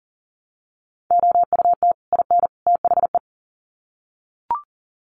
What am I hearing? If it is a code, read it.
Morse code, heard as a keyed tone, about 40 wpm, 700 Hz. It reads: OUT IN THE
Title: Sets of 3 Words 40wpm